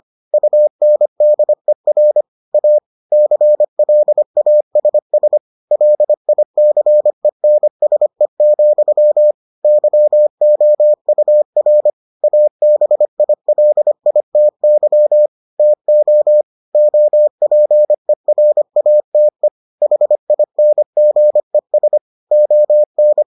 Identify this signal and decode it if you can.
Morse code, heard as a keyed tone, about 25 wpm, 600 Hz. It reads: UNDER A CLASS LICENSE, YOUR ABILITY TO OPERATE HINGES ON